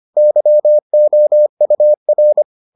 Morse code, heard as a keyed tone, about 25 wpm, 600 Hz. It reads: YOUR